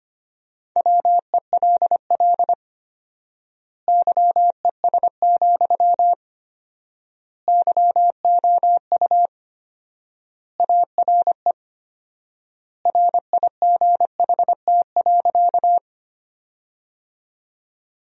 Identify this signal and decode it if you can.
Morse code, heard as a keyed tone, about 25 wpm, 700 Hz. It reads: WELL YES, YOU ARE RIGHT.
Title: 4x Word-Spacing - Sentences from Top 500 Words - 25wpm